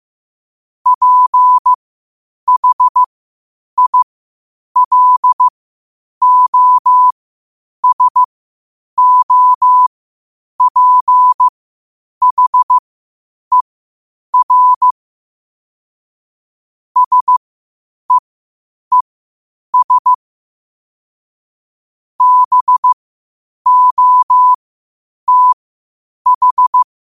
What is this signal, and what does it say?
Morse code, heard as a keyed tone, about 15 wpm, spaced as for 8 wpm, 1000 Hz. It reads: PHILOSOPHER SEES BOTH
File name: qotdfiles / 8wpm